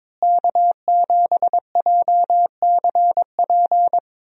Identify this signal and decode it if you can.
Morse code, heard as a keyed tone, about 22 wpm, 700 Hz. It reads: K7JCP
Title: USA Call Signs - 5 Characters - Repeated 3x - 22wpm